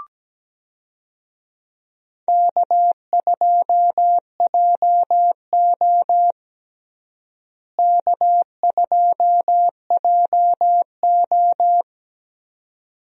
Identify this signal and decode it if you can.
Morse code, heard as a keyed tone, about 17 wpm, 700 Hz. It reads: K2JO K2JO
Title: USA Call Signs - 4 Characters - Repeated 2x - 17wpm